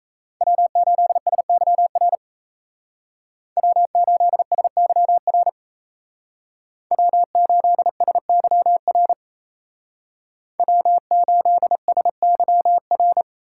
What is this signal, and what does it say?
Morse code, heard as a keyed tone, about 34 wpm, 700 Hz. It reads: W8SYR W8SYR W8SYR W8SYR
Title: Speed Racing - USA 1x3 Call Signs 28wpm